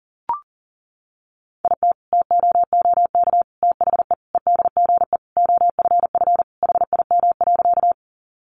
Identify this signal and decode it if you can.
Morse code, heard as a keyed tone, about 40 wpm, 700 Hz. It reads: IT TOOK THE EDGE OFF HIM.
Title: Sentences from Top 1300 Words 40wpm